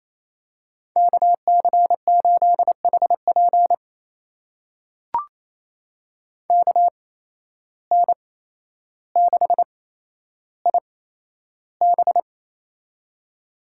Morse code, heard as a keyed tone, about 28 wpm, spaced as for 12 wpm, 700 Hz. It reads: KC8HP KN6IB
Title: USA Call Signs - 5 Characters - 8x Character Spacing - 28wpm